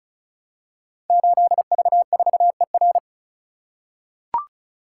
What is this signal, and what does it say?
Morse code, heard as a keyed tone, about 36 wpm, 700 Hz. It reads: 8V4ER
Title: Code Groups - 5 Letter:Numbers - 35wpm